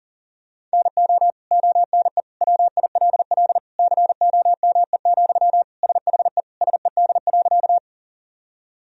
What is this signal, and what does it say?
Morse code, heard as a keyed tone, about 40 wpm, 700 Hz. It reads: NO ONE WILL COME, SHE SED.
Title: Sentences from Top 100 Words and Abbreviations 40wpm